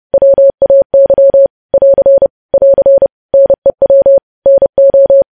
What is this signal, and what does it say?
Morse code, heard as a keyed tone, about 30 wpm, 550 Hz. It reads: WAY <AR> <AR> NEW NO
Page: news-headlines-in-morse-code-at-30-wpm